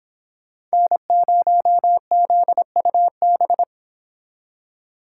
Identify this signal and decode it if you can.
Morse code, heard as a keyed tone, about 25 wpm, 700 Hz. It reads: N0ZUB